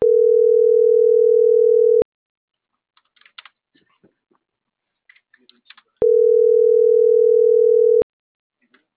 Ringback tone (North America), 440 + 480 Hz.